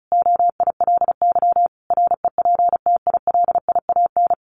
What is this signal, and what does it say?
Morse code, heard as a keyed tone, about 34 wpm, 700 Hz. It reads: OILY REPTILIAN
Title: A Sound Of Thunder 35wpm